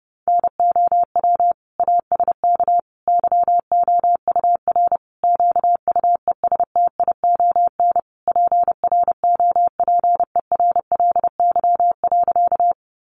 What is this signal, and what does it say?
Morse code, heard as a keyed tone, about 30 wpm, 700 Hz. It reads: NOW ASK YOUR QUESTION PROPERLY.
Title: Sentences from Top 1100 Words 30wpm